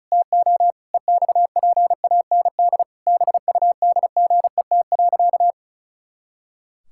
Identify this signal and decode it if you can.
Morse code, heard as a keyed tone, about 34 wpm, 700 Hz. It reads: TO EXPAND BUDGET.
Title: Pangram Challenge v2 - 35wpm